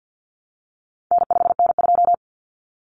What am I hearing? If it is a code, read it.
Morse code, heard as a keyed tone, about 50 wpm, 700 Hz. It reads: N5N2
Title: Common Cut Numbers - 4-Digit 50wpm